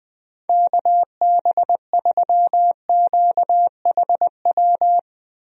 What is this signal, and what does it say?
Morse code, heard as a keyed tone, about 20 wpm, 700 Hz. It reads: KB3QHW
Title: USA Call Signs - 6 Characters - Repeated 2x - 20wpm